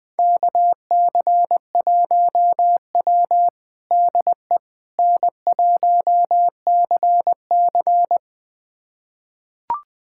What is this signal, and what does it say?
Morse code, heard as a keyed tone, about 21 wpm, 700 Hz. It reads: KC1W DE N1CC
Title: USA Callsign DE Callsign - 4 Characters - 20wpm